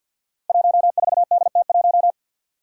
Morse code, heard as a keyed tone, about 51 wpm, 700 Hz. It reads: 14DT1
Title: Five Challenging Letter-Number Combinations 50wpm